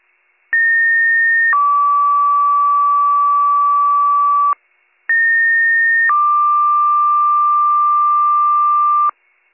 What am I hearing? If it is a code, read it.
Two-tone page, heard as a one-second tone, then a three-second tone, about 1850 then 1150 Hz, about 1850 then 1180 Hz, with noise. It is heard from 0.5 to 9.1 s.